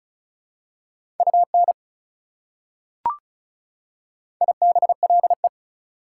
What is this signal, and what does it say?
Morse code, heard as a keyed tone, about 36 wpm, 700 Hz. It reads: UN IBLE
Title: Prefixes and Suffixes 35wpm